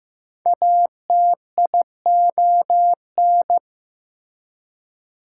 Morse code, heard as a keyed tone, about 15 wpm, 700 Hz. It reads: ATION
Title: Prefixes and Suffixes 15wpm